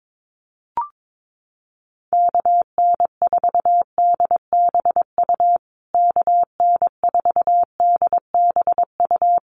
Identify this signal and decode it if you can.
Morse code, heard as a keyed tone, about 22 wpm, 700 Hz. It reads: KN4DBU KN4DBU
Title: USA Call Signs Repeated 3x 22wpm